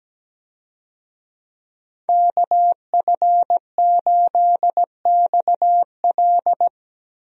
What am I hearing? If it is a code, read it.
Morse code, heard as a keyed tone, about 17 wpm, 700 Hz. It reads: KF8XL
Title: USA Call Signs - 5 Characters - Repeated 2x - 17wpm